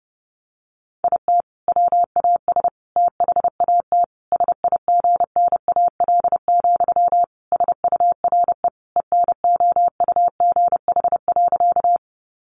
Morse code, heard as a keyed tone, about 30 wpm, 700 Hz. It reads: IT WAS THAT SIGNAL, SURE ENOUGH.
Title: Sentences from Top 1400 Words 30wpm